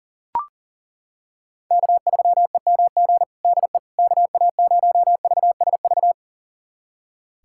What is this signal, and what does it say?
Morse code, heard as a keyed tone, about 40 wpm, 700 Hz. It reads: K3EMG DE KA0VSV
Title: USA Callsign DE Callsign 40wpm